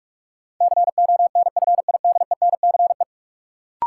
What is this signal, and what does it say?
Morse code, heard as a keyed tone, about 46 wpm, 700 Hz. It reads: CONFIDENCE
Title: Sentences from Top 1500 Words - New Word Drill - 45wpm